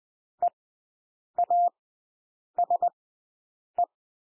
Morse code, heard as a keyed tone, about 20 wpm, spaced as for 8 wpm, 700 Hz. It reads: EASE
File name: cwmp3 / W1 SOUND WORDS 20 wpm